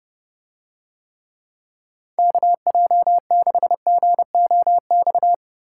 Morse code, heard as a keyed tone, about 28 wpm, 700 Hz. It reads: KJ6GOX